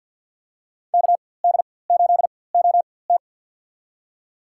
Morse code, heard as a keyed tone, about 48 wpm, spaced as for 23 wpm, 700 Hz. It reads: KD8OT
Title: USA Call Signs - 5 Characters - 4x Character Spacing - 50wpm